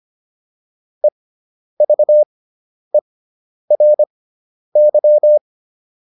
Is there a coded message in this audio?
Yes: Morse code from 1.0 to 5.4 s, heard as a keyed tone, about 25 wpm, spaced as for 10 wpm, 600 Hz. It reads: EVERY